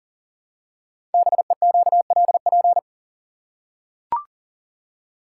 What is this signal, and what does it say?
Morse code, heard as a keyed tone, about 40 wpm, 690 Hz. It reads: BEQLP
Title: Code Groups - 5 Letters - 40wpm